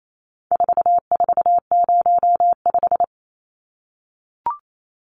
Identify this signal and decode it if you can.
Morse code, heard as a keyed tone, about 28 wpm, 700 Hz. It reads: 4405
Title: Numbers - 4-Digit 28wpm